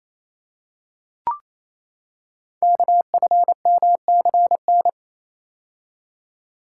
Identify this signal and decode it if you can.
Morse code, heard as a keyed tone, about 27 wpm, 700 Hz. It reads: KFMCN
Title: Code Groups - 5 Letters - 28wpm